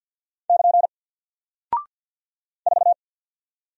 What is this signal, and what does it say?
Morse code, heard as a keyed tone, about 48 wpm, 700 Hz. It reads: <KN> 4
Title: Single Character (Letter:Number:Punctuation:Prosign) 50wpm